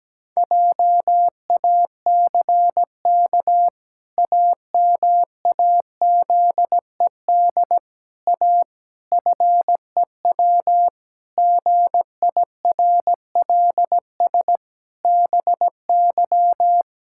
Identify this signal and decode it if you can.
Morse code, heard as a keyed tone, about 17 wpm, 700 Hz. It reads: JACK AMAZED A FEW GIRLS BY
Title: Pangram Challenge v2 - 17wpm